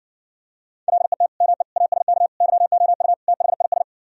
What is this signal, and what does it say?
Morse code, heard as a keyed tone, about 69 wpm, 700 Hz. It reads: LET ME ASK YOU THIS